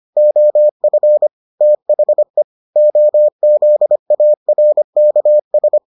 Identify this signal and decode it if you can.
Morse code, heard as a keyed tone, about 25 wpm, 600 Hz. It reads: OF THE OZARKS